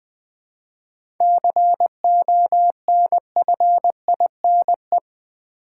Morse code, heard as a keyed tone, about 20 wpm, 700 Hz. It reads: CONFINE
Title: 7 Letter Words - First 3 Letters Given - 20wpm